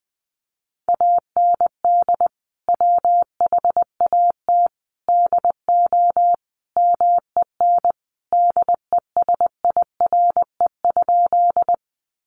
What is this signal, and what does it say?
Morse code, heard as a keyed tone, about 20 wpm, 700 Hz. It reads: AND WHAT DO MEN DESIRE?
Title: Sentences from Top 1100 Words 20wpm